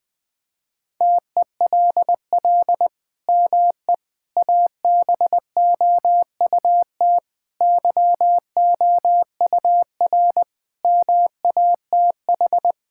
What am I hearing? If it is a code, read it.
Morse code, heard as a keyed tone, about 21 wpm, 700 Hz. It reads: TELL ME ABOUT YOUR MATH
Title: Sentences from Top 1000 Words - Review Part 6 - 20wpm